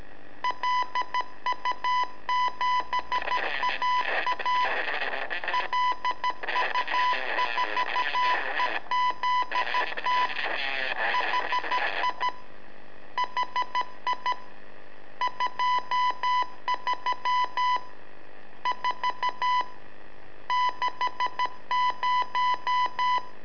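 Morse code, heard as a keyed tone, about 13 wpm, 1000 Hz. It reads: LU7AA LU4EZT HI HI 23 4 60
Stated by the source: Telemetry in Morse from LU7AA and LU4EZ, AMSAT and Turma Mar del Plata, sent from a balloon.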